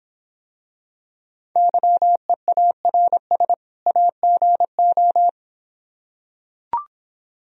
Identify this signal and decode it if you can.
Morse code, heard as a keyed tone, about 26 wpm, 700 Hz. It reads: YEARS AGO